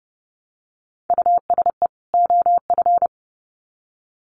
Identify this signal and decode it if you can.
Morse code, heard as a keyed tone, about 30 wpm, 700 Hz. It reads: USE OF
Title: Sets of 2 Words 30wpm